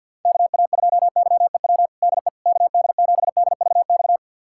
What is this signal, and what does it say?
Morse code, heard as a keyed tone, about 48 wpm, 700 Hz. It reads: KA2YEW DE KD7DVX
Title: USA Callsign DE Callsign - 6 Characters - 50wpm